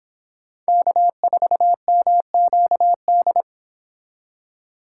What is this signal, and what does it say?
Morse code, heard as a keyed tone, about 25 wpm, 700 Hz. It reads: K4MQD